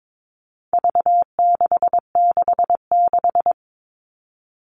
Morse code, heard as a keyed tone, about 22 wpm, 700 Hz. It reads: V666
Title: All Cut Numbers - 4-Digit 22wpm